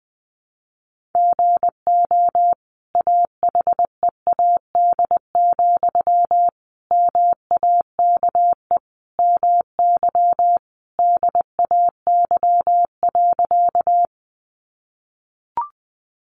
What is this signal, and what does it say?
Morse code, heard as a keyed tone, about 20 wpm, 700 Hz. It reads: GO AHEAD, MAKE MY DAY.
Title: Sentences from Top 1100 Words 20wpm